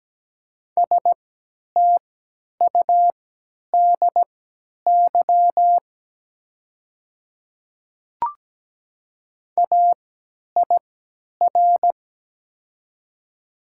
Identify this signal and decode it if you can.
Morse code, heard as a keyed tone, about 17 wpm, spaced as for 10 wpm, 700 Hz. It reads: STUDY AIR
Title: 3x Character Spacing - Top 300 Words - 17wpm